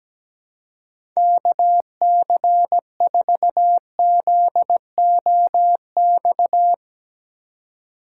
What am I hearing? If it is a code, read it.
Morse code, heard as a keyed tone, about 17 wpm, 700 Hz. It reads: KC4ZOX
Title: USA Call Signs - 6 Characters - 2x Character Spacing - 17wpm